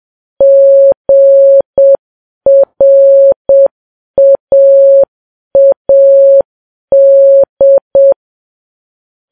Morse code, heard as a keyed tone, about 7 wpm, 550 Hz. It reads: GRAAD
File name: Files Web tls / Morse on9fv / morse7kps